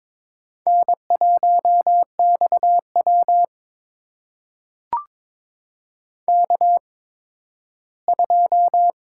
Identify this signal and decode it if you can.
Morse code, heard as a keyed tone, about 22 wpm, spaced as for 13 wpm, 700 Hz. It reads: N1XW K2
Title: USA Call Signs - 4 Characters - 8x Character Spacing - 22wpm